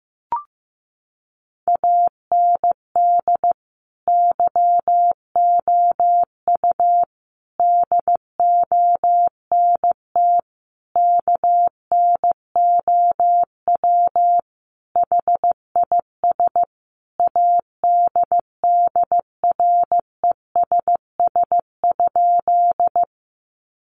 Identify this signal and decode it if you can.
Morse code, heard as a keyed tone, about 14 wpm, 700 Hz. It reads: AND YOU DONT KNOW HIS ADDRESS?
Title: Sentences from Top 1000 Words 15wpm